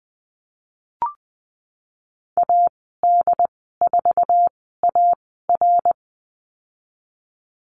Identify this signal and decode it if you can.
Morse code, heard as a keyed tone, about 19 wpm, spaced as for 14 wpm, 700 Hz. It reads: AD4AR
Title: 069 Lesson 12g - Letter d - Call Signs 20wpm